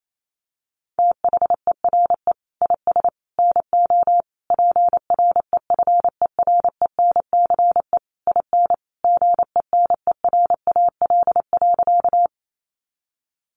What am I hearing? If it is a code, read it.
Morse code, heard as a keyed tone, about 28 wpm, 700 Hz. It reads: THERE IS NO PREFERENCE IN GENERAL.
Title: Sentences from Top 1500 Words 28wpm